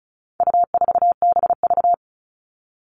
Morse code, heard as a keyed tone, about 34 wpm, 700 Hz. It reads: U4BV
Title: All Cut Numbers - 4-Digit 35wpm